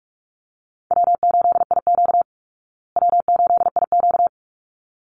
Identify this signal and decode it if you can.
Morse code, heard as a keyed tone, about 44 wpm, 700 Hz. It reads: W8IQ W8IQ
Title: Call Signs Repeated 3x - 45wpm